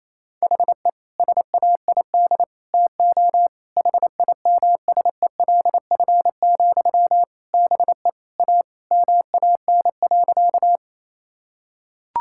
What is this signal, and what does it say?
Morse code, heard as a keyed tone, about 28 wpm, 700 Hz. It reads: HE SAID TO HIMSELF, BE A MAN.
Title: Sentences from Top 1600 Words 28wpm